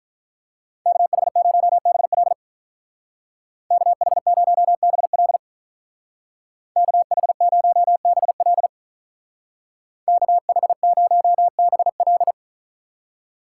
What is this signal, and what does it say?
Morse code, heard as a keyed tone, about 45 wpm, 700 Hz. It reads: KH0BL KH0BL KH0BL KH0BL